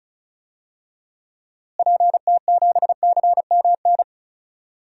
Morse code, heard as a keyed tone, about 36 wpm, 700 Hz. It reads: PT7CMN
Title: DXCC-100 Call Signs - 6 Characters - Repeated 2x - 35wpm